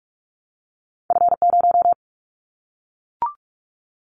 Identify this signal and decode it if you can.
Morse code, heard as a keyed tone, about 46 wpm, 700 Hz. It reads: F0
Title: Callsign Prefixes for Top 25 DXCC - Repeated 3x - 45wpm